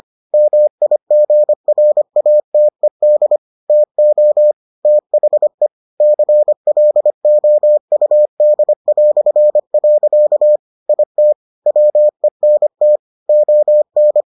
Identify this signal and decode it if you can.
Morse code, heard as a keyed tone, about 25 wpm, 600 Hz. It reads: MIGRATED TO THE CLOUD". IT WENT ON